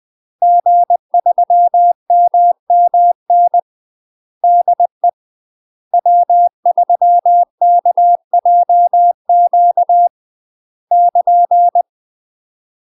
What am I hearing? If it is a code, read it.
Morse code, heard as a keyed tone, about 20 wpm, 700 Hz. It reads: G3MMN DE W3KJQ <KN>
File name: UK MASTER QSOs TRAINING / 20WPM / UK MASTER QSOs 0051